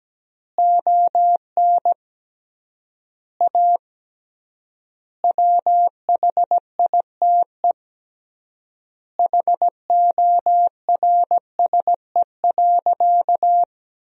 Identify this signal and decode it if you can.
Morse code, heard as a keyed tone, about 17 wpm, 700 Hz. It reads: ON A WHITE HORSE.
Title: 3x Word-Spacing - Sentences from Top 300 Words - 17wpm